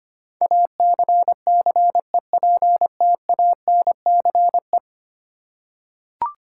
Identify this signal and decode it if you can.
Morse code, heard as a keyed tone, about 25 wpm, 700 Hz. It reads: ACCEPTANCE
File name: Ten Letter Words - 25wpm